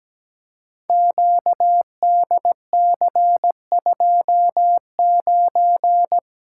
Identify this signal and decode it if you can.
Morse code, heard as a keyed tone, about 17 wpm, 700 Hz. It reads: QDC29
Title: Code Groups - 5 Letter:Numbers - 17wpm